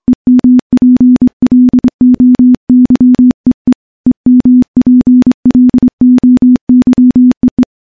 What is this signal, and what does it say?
Morse code, heard as a keyed tone, about 24 wpm, 250 Hz. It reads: EMPLOYEE EMPLOYEE
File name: morse-training-25WPM